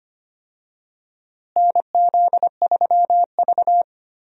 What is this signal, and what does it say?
Morse code, heard as a keyed tone, about 25 wpm, 700 Hz. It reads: NZ3V
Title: USA Call Signs - 4 Characters - Repeated 3x - 25wpm